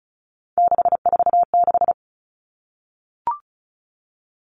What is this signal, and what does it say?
Morse code, heard as a keyed tone, about 36 wpm, 700 Hz. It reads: 646
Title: Three Challenging Letter-Number Combinations 35wpm